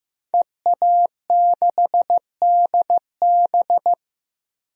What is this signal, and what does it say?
Morse code, heard as a keyed tone, about 15 wpm, 700 Hz. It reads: EA6DB
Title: DXCC-100 Call Signs - 5 Characters - Repeated 3x - 15wpm